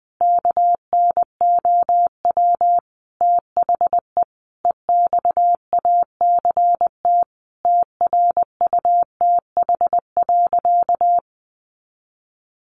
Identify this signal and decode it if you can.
Morse code, heard as a keyed tone, about 20 wpm, 700 Hz. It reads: KNOW THE EXACT TRUTH.
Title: Sentences from Top 1300 Words 20wpm